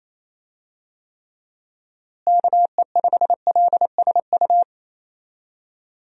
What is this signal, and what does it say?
Morse code, heard as a keyed tone, about 28 wpm, 700 Hz. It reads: KE5LSU
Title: USA Call Signs - 6 Characters - Repeated 2x - 28wpm